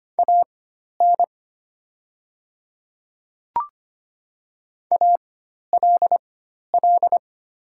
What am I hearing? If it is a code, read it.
Morse code, heard as a keyed tone, about 25 wpm, spaced as for 12 wpm, 700 Hz. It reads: AN ALL
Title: 4x Character Spacing - Top 100 Words - 25wpm